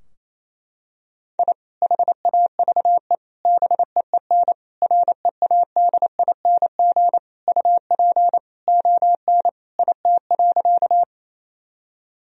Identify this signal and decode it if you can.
Morse code, heard as a keyed tone, about 28 wpm, 700 Hz. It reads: I HAVE BEEN READING UP ON IT.
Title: Sentences from Top 1000 Words - Review Part 3 - 28wpm